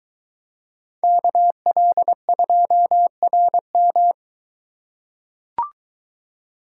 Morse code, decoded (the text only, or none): KL2RM